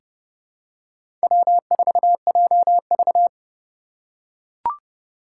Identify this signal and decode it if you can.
Morse code, heard as a keyed tone, about 30 wpm, 700 Hz. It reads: W4JV
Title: USA Call Signs - 4 Characters - 8x Character Spacing - 30wpm